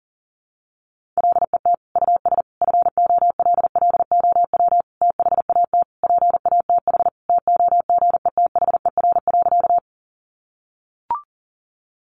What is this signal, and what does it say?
Morse code, heard as a keyed tone, about 40 wpm, 700 Hz. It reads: LET US FOLLOW THAT PATH TOGETHER.
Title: Sentences from Top 1400 Words 40wpm